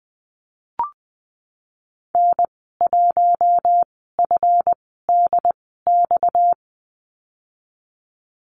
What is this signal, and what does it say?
Morse code, heard as a keyed tone, about 20 wpm, spaced as for 15 wpm, 700 Hz. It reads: N1FDX
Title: 238 Lesson 38g - Letter q - Call Signs 20wpm